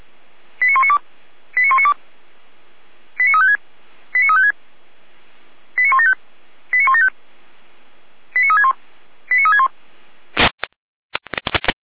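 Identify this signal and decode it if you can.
Five-tone selective call (ZVEI), heard as quick runs of five tones, tones of about 70 ms. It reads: 89192 89192 89356 89356 89175 89175 89361 89361
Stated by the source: ZVEI five-tone call received by SDR.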